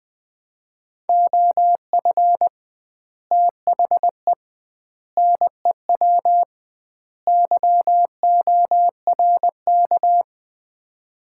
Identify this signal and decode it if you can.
Morse code, heard as a keyed tone, about 20 wpm, 700 Hz. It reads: OF THE NEW YORK
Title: 2x Word-Spacing - Sets of 4 Words - 20wpm